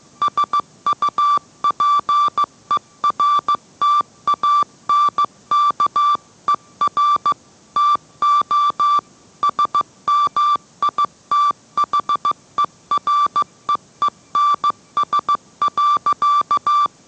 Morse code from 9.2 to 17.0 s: SMITHEREENS.